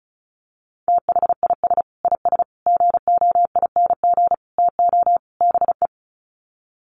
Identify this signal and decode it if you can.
Morse code, heard as a keyed tone, about 34 wpm, 700 Hz. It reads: THIS IS GOING TO BE
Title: Sets of 5 Words 35wpm